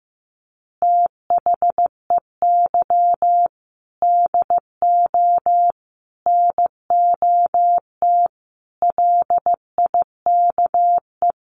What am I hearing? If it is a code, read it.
Morse code, heard as a keyed tone, about 15 wpm, 700 Hz. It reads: THEY DO NOT LIKE
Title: Easy Phrases 15wpm